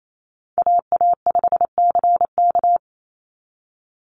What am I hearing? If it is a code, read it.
Morse code, heard as a keyed tone, about 28 wpm, 700 Hz. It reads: AA5CK
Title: NA SOTA Chaser Callsigns 28wpm